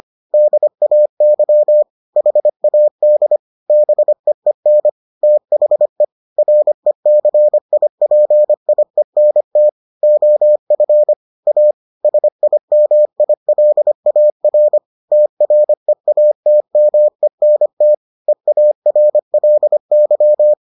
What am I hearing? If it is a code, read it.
Morse code, heard as a keyed tone, about 25 wpm, 600 Hz. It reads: DAY HAD BEEN THE RECIPIENT OF A SIMILAR TREATMENT EARLY